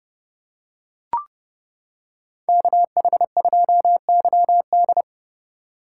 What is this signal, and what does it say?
Morse code, heard as a keyed tone, about 29 wpm, 700 Hz. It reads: KH2YD